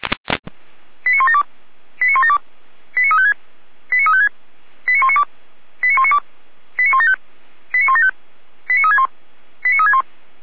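Five-tone selective call (ZVEI), heard as quick runs of five tones, tones of about 80 ms. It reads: 89172 89172 89356 89356 89192 89192 89175 89175 89361 89361
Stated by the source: ZVEI five-tone call received by SDR.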